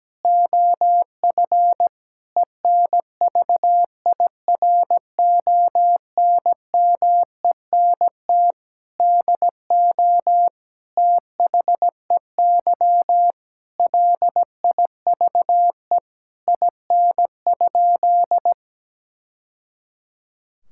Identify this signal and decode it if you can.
Morse code, heard as a keyed tone, about 17 wpm, 700 Hz. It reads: OF ENVIRONMENT DO THEY LIVE IN?
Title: Sentences from Top 1000 Words - Review Part 5 - 17wpm